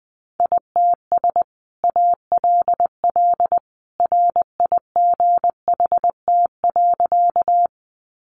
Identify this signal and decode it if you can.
Morse code, heard as a keyed tone, about 20 wpm, 700 Hz. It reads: ITS ALL RIGHT.
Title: Sentences from Top 600 Words 20wpm